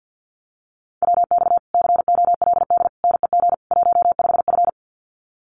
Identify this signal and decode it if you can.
Morse code, heard as a keyed tone, about 50 wpm, 700 Hz. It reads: WX COLD NEG 15F
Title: Common QSO Phrases Encore 50wpm